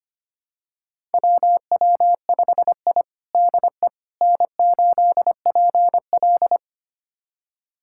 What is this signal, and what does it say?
Morse code, heard as a keyed tone, about 25 wpm, 700 Hz. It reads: WW5I DE N8PL
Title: USA Callsign DE Callsign - 4 Characters - 25wpm